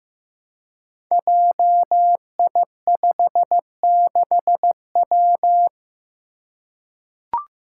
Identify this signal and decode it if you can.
Morse code, heard as a keyed tone, about 15 wpm, 700 Hz. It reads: JI56W
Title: Five Challenging Letter-Number Combinations 15wpm